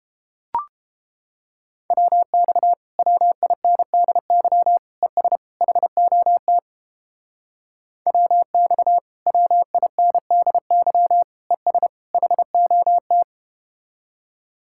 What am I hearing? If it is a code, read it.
Morse code, heard as a keyed tone, about 31 wpm, 700 Hz. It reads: WX WINDY ES HOT WX WINDY ES HOT